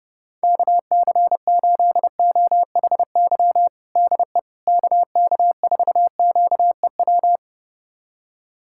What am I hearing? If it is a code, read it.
Morse code, heard as a keyed tone, about 32 wpm, 700 Hz. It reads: KC8OHY DE KK4QEW
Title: USA Callsign DE Callsign 30wpm